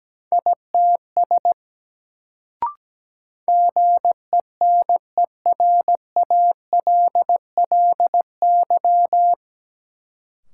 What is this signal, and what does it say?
Morse code, heard as a keyed tone, about 17 wpm, 700 Hz. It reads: ITS GENERALLY
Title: Sentences from Top 600 Words - New Word Drill - 17wpm